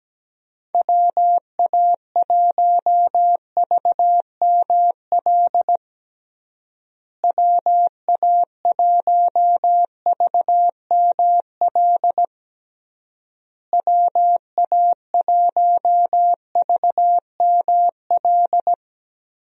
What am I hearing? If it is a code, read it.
Morse code, heard as a keyed tone, about 17 wpm, 700 Hz. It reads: WA1VML WA1VML WA1VML
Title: USA Call Signs - 6 Characters - Repeated 3x - 17wpm